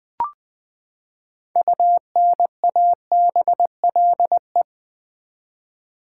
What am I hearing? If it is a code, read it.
Morse code, heard as a keyed tone, about 20 wpm, 700 Hz. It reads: UNABLE
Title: Six Letter Words - 20wpm